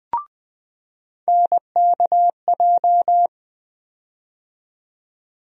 Morse code, heard as a keyed tone, about 20 wpm, 700 Hz. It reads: NKJ
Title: USA Call Sign Postfixes - 3 Letters 20wpm